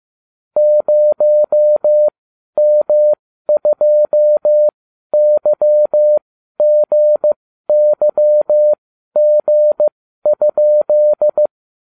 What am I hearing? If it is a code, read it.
Morse code, heard as a keyed tone, about 15 wpm, spaced as for 12 wpm, 600 Hz. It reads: 0M2YGYG?